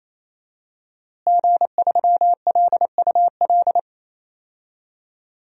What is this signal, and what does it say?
Morse code, heard as a keyed tone, about 27 wpm, 700 Hz. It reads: G3LUL
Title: DXCC-100 Call Signs - 5 Characters - Repeated 3x - 28wpm